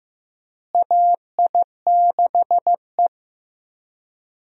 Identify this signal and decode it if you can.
Morse code, heard as a keyed tone, about 15 wpm, 700 Hz. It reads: AI6E
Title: USA Call Signs - 4 Characters - Repeated 3x - 15wpm